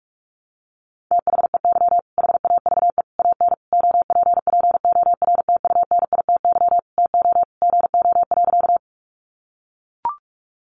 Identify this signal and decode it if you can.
Morse code, heard as a keyed tone, about 44 wpm, 700 Hz. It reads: THEY HAVE AN OPPORTUNITY TO GO.
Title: Sentences from Top 1000 Words 45wpm